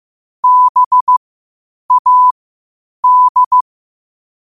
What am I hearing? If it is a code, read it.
Morse code, heard as a keyed tone, about 15 wpm, spaced as for 8 wpm, 1000 Hz. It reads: BAD
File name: qotdfiles / 8wpm